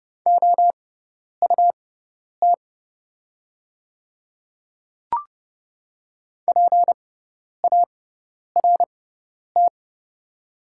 Morse code, heard as a keyed tone, about 30 wpm, spaced as for 10 wpm, 700 Hz. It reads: OUT PART
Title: 6x Character Spacing - Top 200 Words - 30wpm